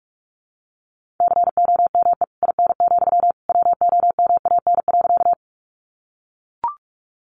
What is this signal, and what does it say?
Morse code, heard as a keyed tone, about 44 wpm, 700 Hz. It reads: COME IN, WOMAN.
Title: Sentences from Top 1100 Words 45wpm